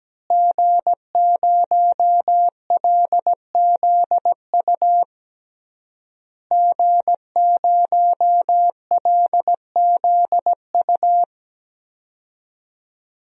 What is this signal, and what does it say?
Morse code, heard as a keyed tone, about 17 wpm, 700 Hz. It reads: G0LZU G0LZU
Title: DXCC-100 Call Signs - 5 Characters - Repeated 3x - 17wpm